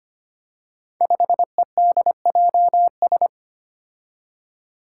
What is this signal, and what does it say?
Morse code, heard as a keyed tone, about 25 wpm, 700 Hz. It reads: 5EDJS
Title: Five Challenging Letter-Number Combinations 25wpm